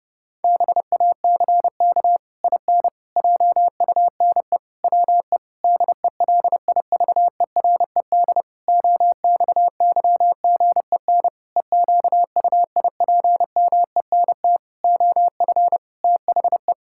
Morse code, heard as a keyed tone, about 32 wpm, 700 Hz. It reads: BACK IN JUNE WE DELIVERED OXYGEN EQUIPMENT OF THE
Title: Pangram Challenge v2 - 30wpm